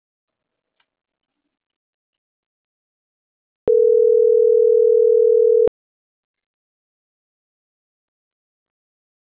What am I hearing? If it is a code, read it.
Ringback tone (North America), 440 + 480 Hz.